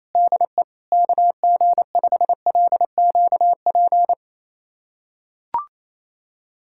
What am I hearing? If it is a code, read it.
Morse code, heard as a keyed tone, about 28 wpm, 700 Hz. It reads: DE KG5LQP
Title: USA Callsign DE Callsign - 6 Characters - 28wpm